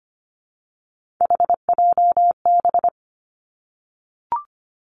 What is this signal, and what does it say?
Morse code, heard as a keyed tone, about 25 wpm, 700 Hz. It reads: HJB
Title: Three Challenging Letter-Number Combinations 25wpm